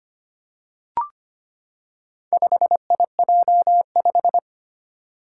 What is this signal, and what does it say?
Morse code, heard as a keyed tone, about 25 wpm, 700 Hz. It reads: 5IJ5
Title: Four Challenging Letter-Number Combinations 25wpm